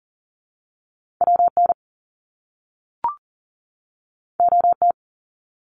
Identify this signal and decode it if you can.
Morse code, heard as a keyed tone, about 40 wpm, 700 Hz. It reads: WN OT
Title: Top Two Letter Combinations in Words 40wpm